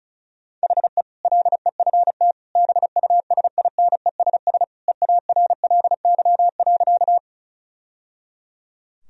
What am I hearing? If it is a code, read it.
Morse code, heard as a keyed tone, about 36 wpm, 700 Hz. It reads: HE LEFT BUSINESS EARLY.
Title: Sentences from Top 1000 Words - Review Part 2 - 35wpm